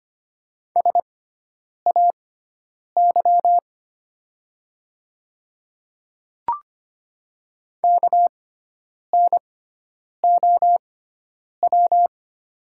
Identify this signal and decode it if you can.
Morse code, heard as a keyed tone, about 25 wpm, spaced as for 9 wpm, 690 Hz. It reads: SAY KNOW
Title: 6x Character Spacing - Top 200 Words - 25wpm